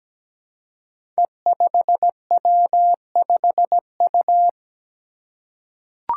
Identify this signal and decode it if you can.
Morse code, heard as a keyed tone, about 17 wpm, 700 Hz. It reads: E5W5U
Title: Five Challenging Letter-Number Combinations 17wpm